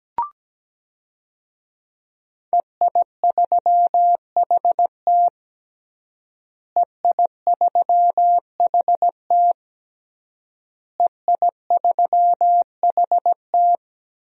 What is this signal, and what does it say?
Morse code, heard as a keyed tone, about 17 wpm, 700 Hz. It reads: EI3HT EI3HT EI3HT
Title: DXCC-100 Call Signs - 5 Characters - Repeated 3x - 17wpm